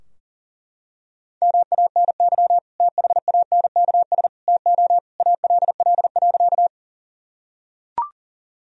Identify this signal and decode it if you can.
Morse code, heard as a keyed tone, about 40 wpm, 700 Hz. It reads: MANY THANKS TO ALL.
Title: Sentences from Top 1000 Words - Review Part 3 - 40wpm